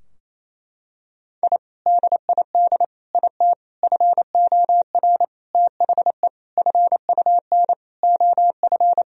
Morse code, heard as a keyed tone, about 28 wpm, 700 Hz. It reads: I DID IT FOR THE FUN OF
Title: Sentences from Top 1000 Words - Review Part 4 - 28wpm